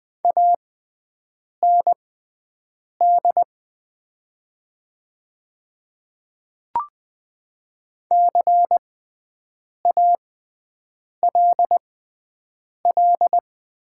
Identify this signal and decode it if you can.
Morse code, heard as a keyed tone, about 20 wpm, spaced as for 7 wpm, 700 Hz. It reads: AND CALL
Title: 6x Character Spacing - Top 100 Words - 20wpm